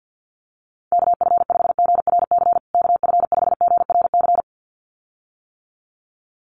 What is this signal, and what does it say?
Morse code, heard as a keyed tone, about 49 wpm, 700 Hz. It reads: KF5GRC KF5GRC
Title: USA Call Signs Repeated 2x 50wpm